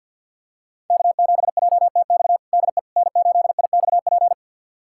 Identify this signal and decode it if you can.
Morse code, heard as a keyed tone, about 47 wpm, 700 Hz. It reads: K7JTX DE N8IXP
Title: USA Callsign DE Callsign - 5 Characters - 50wpm